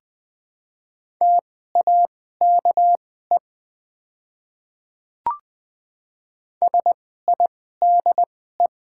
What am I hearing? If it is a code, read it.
Morse code, heard as a keyed tone, about 20 wpm, spaced as for 15 wpm, 700 Hz. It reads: TAKE SIDE
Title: 2x Character Spacing - Top 200 Words - 20wpm